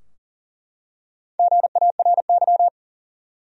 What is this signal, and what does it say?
Morse code, heard as a keyed tone, about 40 wpm, 700 Hz. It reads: GARY
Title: USA Common Names - Max 4 Letters - 40wpm